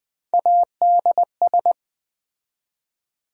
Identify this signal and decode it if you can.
Morse code, heard as a keyed tone, about 20 wpm, 700 Hz. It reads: ADS